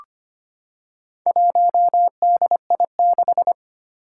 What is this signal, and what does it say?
Morse code, heard as a keyed tone, about 25 wpm, 700 Hz. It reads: 1DI6
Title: Four Challenging Letter-Number Combinations 25wpm